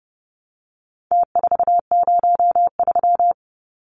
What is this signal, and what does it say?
Morse code, heard as a keyed tone, about 30 wpm, 700 Hz. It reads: T403
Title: Numbers - Contest Style - 4-Digit 30wpm